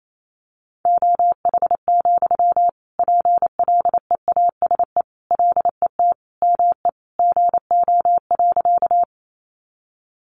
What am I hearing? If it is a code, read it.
Morse code, heard as a keyed tone, about 28 wpm, 700 Hz. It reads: OH, PLEASE LET ME GO.
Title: Sentences from Top 1500 Words 28wpm